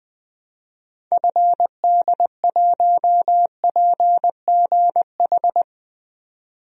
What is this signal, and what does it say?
Morse code, heard as a keyed tone, about 20 wpm, 700 Hz. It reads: FD1PGH
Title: DXCC-100 Call Signs - 6 Characters - 20wpm